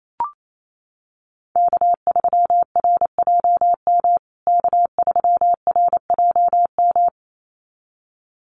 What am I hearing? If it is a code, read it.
Morse code, heard as a keyed tone, about 28 wpm, 700 Hz. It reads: K3RJM K3RJM
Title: USA Call Signs Repeated 2x 28wpm